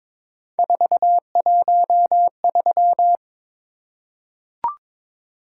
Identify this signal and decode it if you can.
Morse code, heard as a keyed tone, about 22 wpm, 700 Hz. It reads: 413
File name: practice / 22wpm / Three Letter-Number Combinations - 22wpm